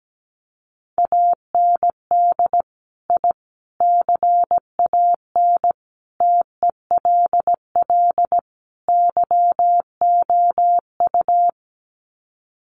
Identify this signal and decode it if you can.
Morse code, heard as a keyed tone, about 17 wpm, 700 Hz. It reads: AND I CAN TELL YOU
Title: Sets of 5 Words Encore2 17wpm